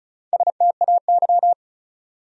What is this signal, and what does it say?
Morse code, heard as a keyed tone, about 44 wpm, 700 Hz. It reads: STAY
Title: Speed-Racing Top 500 Words 35wpm